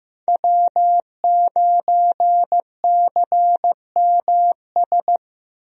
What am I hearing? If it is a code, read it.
Morse code, heard as a keyed tone, about 16 wpm, 700 Hz. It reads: W9CMS
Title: USA Call Signs - 5 Characters - 15wpm